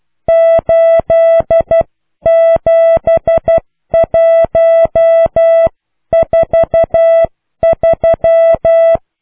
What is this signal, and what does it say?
Morse code, heard as a keyed tone, about 12 wpm, 660 Hz. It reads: 87143